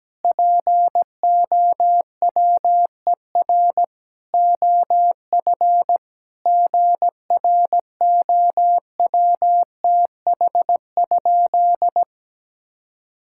Morse code, heard as a keyed tone, about 17 wpm, 700 Hz. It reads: POWER OF GROWTH?
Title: Sentences from Top 1000 Words - Review Part 5 - 17wpm